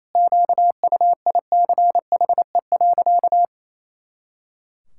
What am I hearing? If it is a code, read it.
Morse code, heard as a keyed tone, about 28 wpm, 700 Hz. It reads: QUICHE.
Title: Pangram Challenge v2 - 28wpm